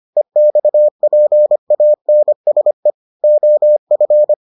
Morse code, heard as a keyed tone, about 25 wpm, 600 Hz. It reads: EXPANSE OF